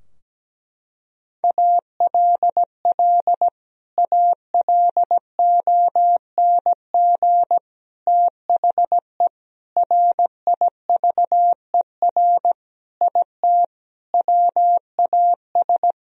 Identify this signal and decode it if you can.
Morse code, heard as a keyed tone, about 17 wpm, 700 Hz. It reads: ALL ALONG THE RIVER IT WAS
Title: Sentences from Top 1000 Words - Review Part 2 - 17wpm